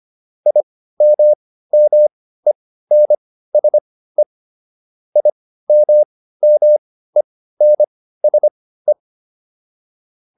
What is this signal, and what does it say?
Morse code, heard as a keyed tone, about 25 wpm, spaced as for 15 wpm, 600 Hz. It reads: IMMENSE IMMENSE